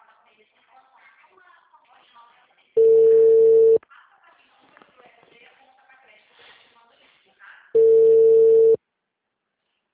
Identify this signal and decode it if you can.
Ringback tone (China, 450 Hz), with speech.
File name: out- -2016- - - .2346